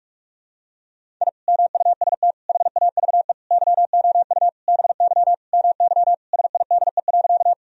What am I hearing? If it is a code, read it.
Morse code, heard as a keyed tone, about 45 wpm, 700 Hz. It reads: I MUST HAVE YOU BY MY SIDE.
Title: Sentences from Top 1000 Words - Review Part 1 - 45wpm